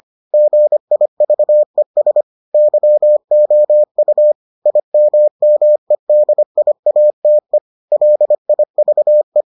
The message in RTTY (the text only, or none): none